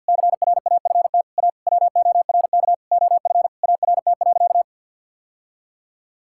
Morse code, heard as a keyed tone, about 51 wpm, 700 Hz. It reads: CRAFT A WORK OF ART.
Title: Sentences from Top 1000 Words - Review Part 6 - 50wpm